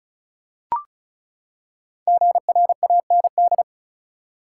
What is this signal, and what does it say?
Morse code, heard as a keyed tone, about 34 wpm, 700 Hz. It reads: GRAND
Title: Five Letter Words - 35wpm